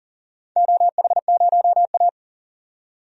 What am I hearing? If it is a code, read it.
Morse code, heard as a keyed tone, about 38 wpm, 700 Hz. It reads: OH0A
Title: DXCC-100 Call Signs - 4 Characters - 40wpm